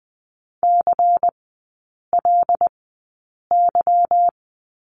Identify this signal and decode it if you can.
Morse code, heard as a keyed tone, about 20 wpm, spaced as for 8 wpm, 690 Hz. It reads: CLY
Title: 205 Lesson 33e - Letter j - Three Characters 20wpm